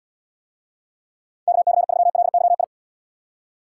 Morse code, heard as a keyed tone, about 72 wpm, 700 Hz. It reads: KC3LYI